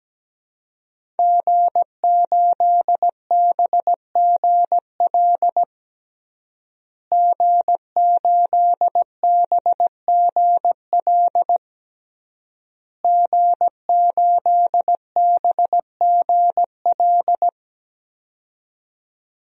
Morse code, heard as a keyed tone, about 17 wpm, 700 Hz. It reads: G8BGL G8BGL G8BGL
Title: DXCC-100 Call Signs - 5 Characters - Repeated 3x - 17wpm